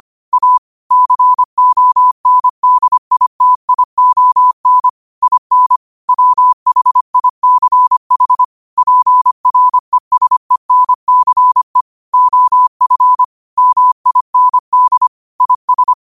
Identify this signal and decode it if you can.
Morse code, heard as a keyed tone, about 25 wpm, 1000 Hz. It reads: A CONDITION IN WHICH PRESENCE OF MIND IS
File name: qotdfiles / 25wpm